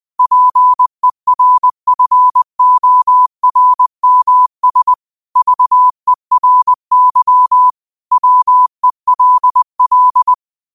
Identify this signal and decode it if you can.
Morse code, heard as a keyed tone, about 20 wpm, 1000 Hz. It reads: PERFORMS VERY WELL